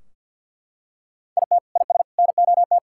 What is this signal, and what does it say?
Morse code, heard as a keyed tone, about 52 wpm, 700 Hz. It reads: IT IS NOT
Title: Sentences from Top 1600 Words 50wpm